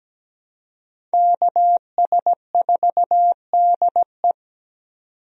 Morse code, heard as a keyed tone, about 17 wpm, 700 Hz. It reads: KS4DE
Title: USA Call Signs - 5 Characters - 17wpm